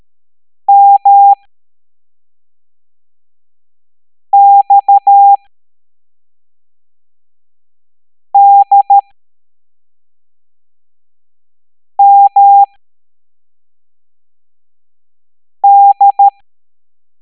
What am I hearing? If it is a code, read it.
Morse code, heard as a keyed tone, about 13 wpm, spaced as for 3 wpm, 800 Hz. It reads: MXDMD
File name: hamdata / L02-EX01-MDX